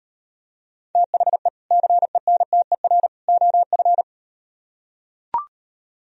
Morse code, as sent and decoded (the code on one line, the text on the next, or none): - .... / -.-. -. - . .-. / --- ..-.
THE CENTER OF